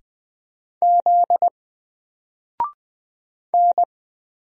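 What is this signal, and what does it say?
Morse code, heard as a keyed tone, about 20 wpm, 700 Hz. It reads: Z N
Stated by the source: Single letter and number Morse drill at 20 wpm, from N4LCD.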